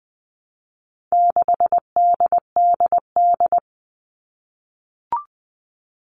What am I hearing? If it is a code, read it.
Morse code, heard as a keyed tone, about 20 wpm, 700 Hz. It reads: 6DDD
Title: All Cut Numbers - 4-Digit 20wpm